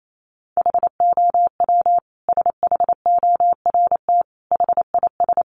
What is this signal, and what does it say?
Morse code, heard as a keyed tone, about 28 wpm, 700 Hz. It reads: HOW SHORT HIS
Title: Sentences from Top 400 Words 28wpm